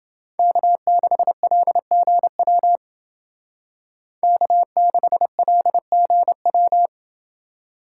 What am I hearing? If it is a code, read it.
Morse code, heard as a keyed tone, about 28 wpm, 700 Hz. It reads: K6LGW K6LGW